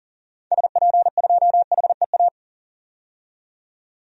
Morse code, heard as a keyed tone, about 40 wpm, 700 Hz. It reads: SP2HEA
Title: DXCC-100 Call Signs - 6 Characters - 40wpm